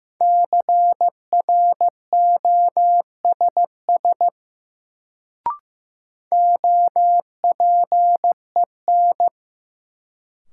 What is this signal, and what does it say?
Morse code, heard as a keyed tone, about 15 wpm, 700 Hz. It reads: CROSS OPEN
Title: Sentences from Top 300 Words - New Word Drill - 15wpm